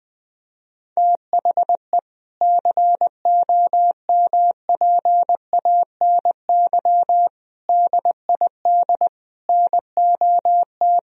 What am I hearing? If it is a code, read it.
Morse code, heard as a keyed tone, about 19 wpm, 700 Hz. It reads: THE COMPANY DID NOT
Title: Sentences from Top 1000 Words - Review Part 3 - 20wpm